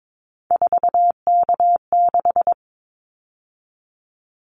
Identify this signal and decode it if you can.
Morse code, heard as a keyed tone, about 22 wpm, 700 Hz. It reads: 4K6
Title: Callsign Prefixes for Top 100 DXCC - 22wpm